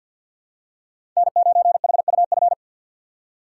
Morse code, heard as a keyed tone, about 51 wpm, 700 Hz. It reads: N9HUF
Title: USA Call Signs - 3x Character Spacing - 50wpm